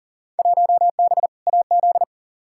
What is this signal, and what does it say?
Morse code, heard as a keyed tone, about 40 wpm, 700 Hz. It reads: 1B AZ
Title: Field Day Exchanges Encore - 40wpm